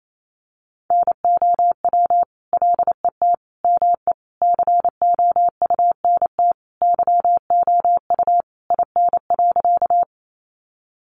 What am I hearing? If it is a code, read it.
Morse code, heard as a keyed tone, about 28 wpm, 700 Hz. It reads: NOW LET ME COUNT YOU IN.
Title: Sentences from Top 1500 Words 28wpm